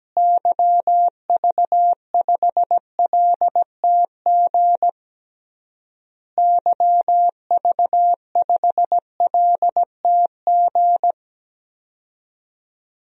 Morse code, heard as a keyed tone, about 17 wpm, 700 Hz. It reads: YV5LTG YV5LTG
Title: DXCC-100 Call Signs - 6 Characters - Repeated 3x - 17wpm